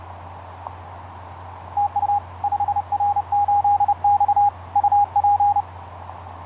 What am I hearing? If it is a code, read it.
Morse code, heard as a keyed tone, about 31 wpm, 820 Hz. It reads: TU 5R8X UP
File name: 5r8x iz5ebl etc 12m cw 07nov2010 1003